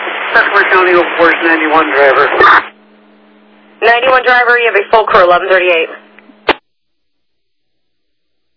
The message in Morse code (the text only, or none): none